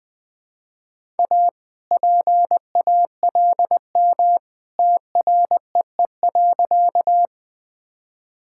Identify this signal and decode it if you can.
Morse code, heard as a keyed tone, about 20 wpm, 700 Hz. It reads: A PALM TREE.
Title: All-Age Appropriate Jokes - 20wpm